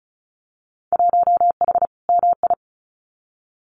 Morse code, heard as a keyed tone, about 36 wpm, 700 Hz. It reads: 1H MI
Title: Winter Field Day Exchanges 35wpm